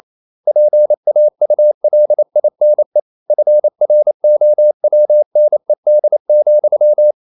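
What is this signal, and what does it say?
Morse code, heard as a keyed tone, about 28 wpm, 600 Hz. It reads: PAULINE FROWNED,
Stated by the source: Morse code practice at 28 wpm, text of Ken Follett's novel Never in English, from CW med Gnister.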